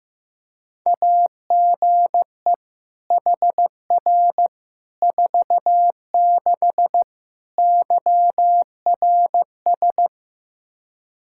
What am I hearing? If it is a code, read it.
Morse code, heard as a keyed tone, about 15 wpm, 700 Hz. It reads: AGE HR 46 YRS
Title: Practice QSO - Exchange 3 - More Info - 15wpm